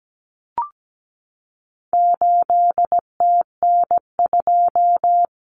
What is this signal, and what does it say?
Morse code, heard as a keyed tone, about 17 wpm, 700 Hz. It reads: 8TN2
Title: Common Cut Numbers - 4-Digit 17wpm